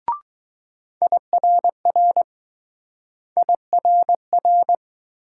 Morse code, heard as a keyed tone, about 20 wpm, 700 Hz. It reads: IRR IRR